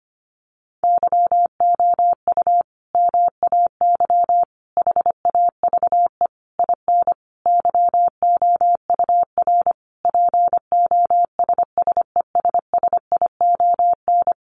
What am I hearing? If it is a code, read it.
Morse code, heard as a keyed tone, about 25 wpm, 700 Hz. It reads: YOU MAY HAVE IN YOUR POSSESSION
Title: Sentences from Top 1400 Words 25wpm